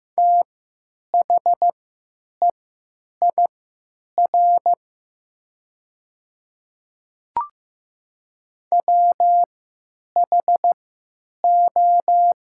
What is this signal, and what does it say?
Morse code, heard as a keyed tone, about 15 wpm, spaced as for 9 wpm, 700 Hz. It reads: THEIR WHO
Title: 3x Character Spacing - Top 100 Words - 15wpm